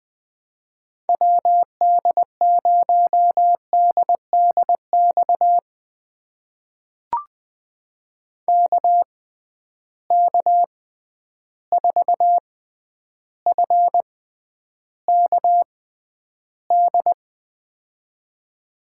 Morse code, heard as a keyed tone, about 20 wpm, spaced as for 10 wpm, 700 Hz. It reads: WD0DDX KK4FKD
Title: USA Call Signs - 6 Characters - 6x Character Spacing - 20wpm